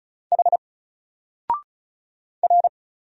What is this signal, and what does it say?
Morse code, heard as a keyed tone, about 34 wpm, 700 Hz. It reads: H R
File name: Single-Letter-Number-35wpm